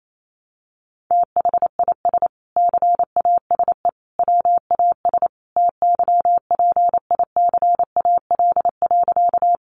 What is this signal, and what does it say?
Morse code, heard as a keyed tone, about 28 wpm, 700 Hz. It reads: THIS CASE WAS TYPICAL.
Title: Sentences from Top 1100 Words 28wpm